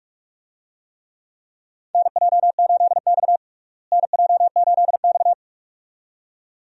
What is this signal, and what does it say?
Morse code, heard as a keyed tone, about 46 wpm, 700 Hz. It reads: NJ8X NJ8X
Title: USA Call Signs - 4 Characters - Repeated 2x - 45wpm